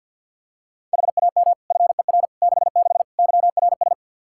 Morse code, heard as a keyed tone, about 53 wpm, 700 Hz. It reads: HAM FER 66 YRS
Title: Practice QSO - Exchange 3 - More Info - 50wpm